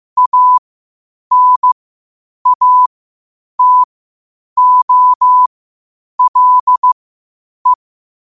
Morse code, heard as a keyed tone, about 15 wpm, spaced as for 8 wpm, 1000 Hz. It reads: ANATOLE
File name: qotdfiles / 8wpm